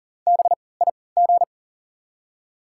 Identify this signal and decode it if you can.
Morse code, heard as a keyed tone, about 40 wpm, spaced as for 23 wpm, 700 Hz. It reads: BIG
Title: 3x Character Spacing - Top 300 Words - 40wpm